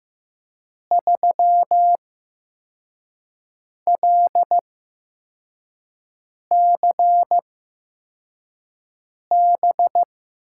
Morse code, heard as a keyed tone, about 15 wpm, spaced as for 4 wpm, 700 Hz. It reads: 3LCB
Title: USA Call Signs - 6 Characters - 8x Character Spacing - 15wpm